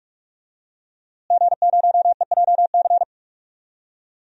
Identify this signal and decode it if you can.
Morse code, heard as a keyed tone, about 46 wpm, 700 Hz. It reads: G0EJC
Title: DXCC-100 Call Signs - 5 Characters - Repeated 3x - 45wpm